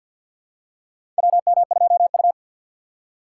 Morse code, heard as a keyed tone, about 47 wpm, 700 Hz. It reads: WM2U
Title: USA Call Signs - 4x Character Spacing - 50wpm